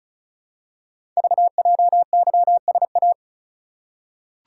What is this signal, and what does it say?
Morse code, heard as a keyed tone, about 36 wpm, 690 Hz. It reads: VJYSA
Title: Code Groups - 5 Letters - 35wpm